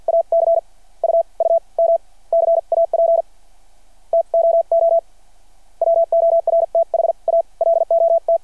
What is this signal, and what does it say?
Morse code, heard as a keyed tone, about 52 wpm, spaced as for 38 wpm, 690 Hz. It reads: ACUUM CAP TOO WORTH A LOT
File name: archived / CRSnet